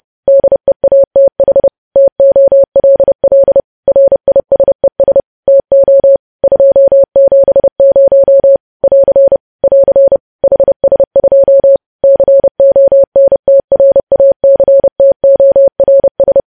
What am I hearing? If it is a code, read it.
Morse code, heard as a keyed tone, about 30 wpm, 550 Hz. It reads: DEATH TOLL RISES TO 270 <AR> <AR> HS2 CONTRACTORS
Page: news headlines in morse code at 30 wpm